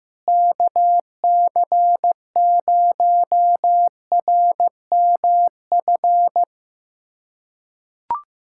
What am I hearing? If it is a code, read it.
Morse code, heard as a keyed tone, about 16 wpm, 700 Hz. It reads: KC0RMF
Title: USA Callsign DE Callsign 15wpm